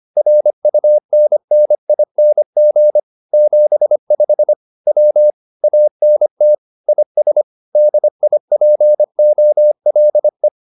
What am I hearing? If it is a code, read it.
Morse code, heard as a keyed tone, about 25 wpm, 600 Hz. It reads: RUNNING 75 W ANT IS DIPOLE